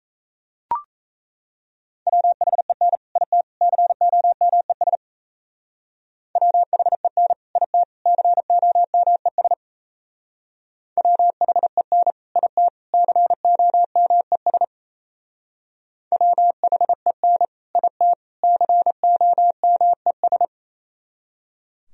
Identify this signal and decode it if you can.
Morse code, heard as a keyed tone, about 35 wpm, 700 Hz. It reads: WHEN IT COMES WHEN IT COMES WHEN IT COMES WHEN IT COMES
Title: Speed-Racing - Sets of 3 Words 28wpm